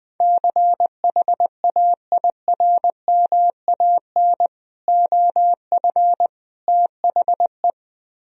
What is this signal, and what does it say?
Morse code, heard as a keyed tone, about 20 wpm, 700 Hz. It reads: CHAIRMAN OF THE
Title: Speed-Racing - Sets of 3 Words 17wpm